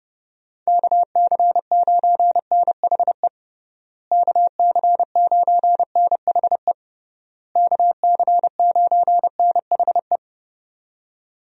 Morse code, heard as a keyed tone, about 30 wpm, 700 Hz. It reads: KC9NHE KC9NHE KC9NHE
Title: USA Call Signs - 6 Characters - Repeated 3x - 30wpm